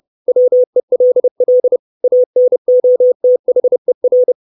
Morse code, heard as a keyed tone, about 30 wpm, 500 Hz. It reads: WELL ANOTHER
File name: 632 Words 30WPM